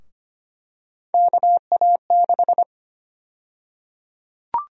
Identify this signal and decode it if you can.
Morse code, heard as a keyed tone, about 25 wpm, 700 Hz. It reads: KA6
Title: USA Call Sign Prefixes - Repeated 2x - 25wpm